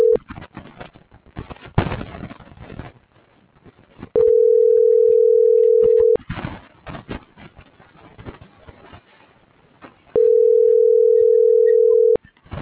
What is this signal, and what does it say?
Ringback tone (North America), 440 + 480 Hz.